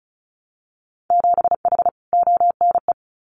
Morse code, heard as a keyed tone, about 36 wpm, 700 Hz. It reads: 7H ONE
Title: Winter Field Day Exchanges Encore 35wpm